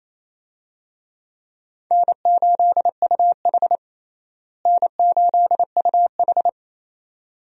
Morse code, heard as a keyed tone, about 28 wpm, 700 Hz. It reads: N8UH N8UH
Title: USA Call Signs - 4 Characters - Repeated 2x - 28wpm